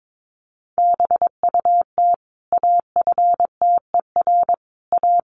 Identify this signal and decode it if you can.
Morse code, heard as a keyed tone, about 22 wpm, 700 Hz. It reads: BUT AFTER A
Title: Sentences from Top 1500 Words 22wpm